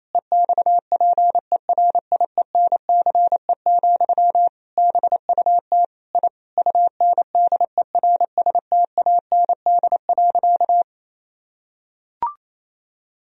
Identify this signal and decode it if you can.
Morse code, heard as a keyed tone, about 28 wpm, 700 Hz. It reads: EXPERIENCE, BUT I UNDERSTAND.
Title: Sentences from Top 1000 Words - Review Part 4 - 28wpm